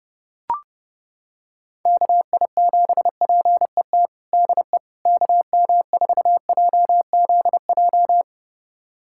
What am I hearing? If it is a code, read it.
Morse code, heard as a keyed tone, about 30 wpm, 700 Hz. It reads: KI7PET DE KM4JZJ
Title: USA Callsign DE Callsign - 6 Characters - 30wpm